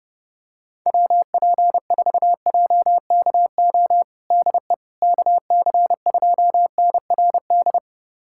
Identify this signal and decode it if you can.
Morse code, heard as a keyed tone, about 30 wpm, 700 Hz. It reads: WP4JKO DE KC2NRD
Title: USA Callsign DE Callsign 30wpm